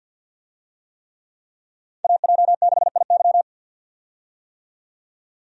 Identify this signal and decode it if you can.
Morse code, heard as a keyed tone, about 52 wpm, 700 Hz. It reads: AJ6IY
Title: USA Call Signs - 5 Characters - 50wpm